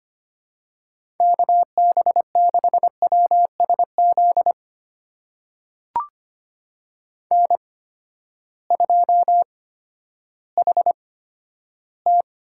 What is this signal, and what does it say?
Morse code, heard as a keyed tone, about 25 wpm, spaced as for 12 wpm, 700 Hz. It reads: KB6WSZ N2HT
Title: USA Call Signs - 8x Character Spacing - 25wpm